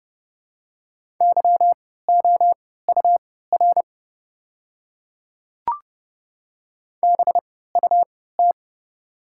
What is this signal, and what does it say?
Morse code, heard as a keyed tone, about 30 wpm, spaced as for 17 wpm, 690 Hz. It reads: YOUR BUT
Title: 3x Character Spacing - Top 200 Words - 30wpm